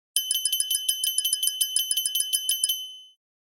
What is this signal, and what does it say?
There is no code.